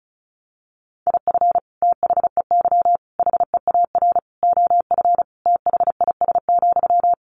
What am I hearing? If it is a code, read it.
Morse code, heard as a keyed tone, about 34 wpm, 700 Hz. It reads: IF THEY HEAR OF THIS,
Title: Sentences from Top 400 Words 35wpm